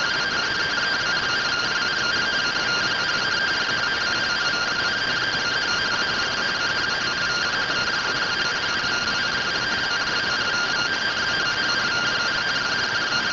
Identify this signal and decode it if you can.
Two-tone teleprinter signal, about 170 Hz shift.